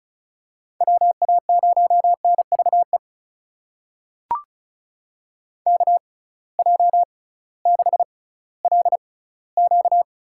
Morse code, heard as a keyed tone, about 34 wpm, spaced as for 19 wpm, 700 Hz. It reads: WA0NVE KJ6LQ
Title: USA Call Signs - 6 Characters - 6x Character Spacing - 35wpm